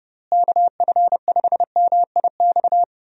Morse code, heard as a keyed tone, about 32 wpm, 700 Hz. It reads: KF5MIX
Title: USA Call Signs - 6 Characters - Repeated 3x - 30wpm